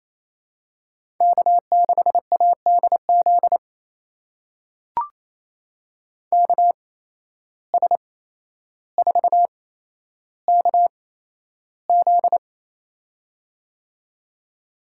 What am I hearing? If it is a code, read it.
Morse code, heard as a keyed tone, about 28 wpm, spaced as for 12 wpm, 700 Hz. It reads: K6ADZ KS4KZ
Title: USA Call Signs - 8x Character Spacing - 28wpm